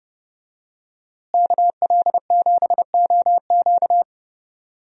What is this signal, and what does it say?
Morse code, heard as a keyed tone, about 31 wpm, 700 Hz. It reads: KL7OQ
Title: DXCC-100 Call Signs - 5 Characters - Repeated 3x - 30wpm